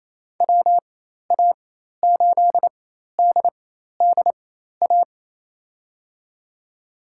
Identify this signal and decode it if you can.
Morse code, heard as a keyed tone, about 28 wpm, spaced as for 13 wpm, 700 Hz. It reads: WA8DDA
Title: USA Call Signs - 6 Characters - 4x Character Spacing - 28wpm